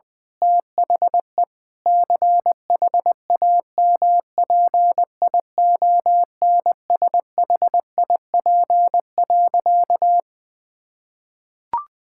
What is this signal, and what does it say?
Morse code, heard as a keyed tone, about 20 wpm, 700 Hz. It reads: THE CHAMPIONSHIP.
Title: Sentences with 8-Word Phrases - Spoken Beginning - 20wpm